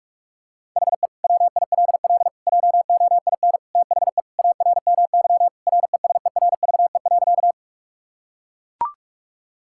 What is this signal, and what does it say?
Morse code, heard as a keyed tone, about 41 wpm, 700 Hz. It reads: HE WILL JOIN THE ARMY RESERVE.
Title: Sentences from Top 1000 Words - Review Part 6 - 45wpm